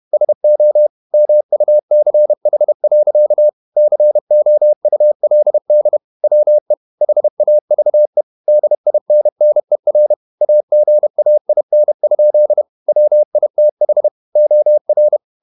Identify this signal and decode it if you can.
Morse code, heard as a keyed tone, about 31 wpm, 600 Hz. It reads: SO MUCH. COULD WE HAVE DINNER AGAIN? WITH OR
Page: Never 31wpm | CW med Gnister